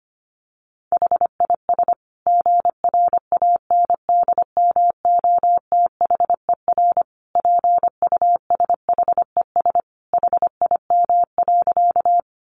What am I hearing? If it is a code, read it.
Morse code, heard as a keyed tone, about 25 wpm, 700 Hz. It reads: HIS GRANDMOTHER PUSHES HIM.
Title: Additional Easy Phrases 25wpm